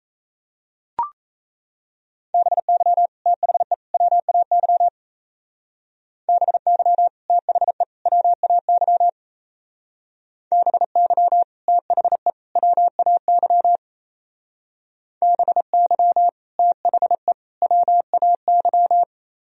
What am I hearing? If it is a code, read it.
Morse code, heard as a keyed tone, about 34 wpm, 700 Hz. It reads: BY THE WAY BY THE WAY BY THE WAY BY THE WAY
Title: Speed-Racing - Sets of 3 Words 28wpm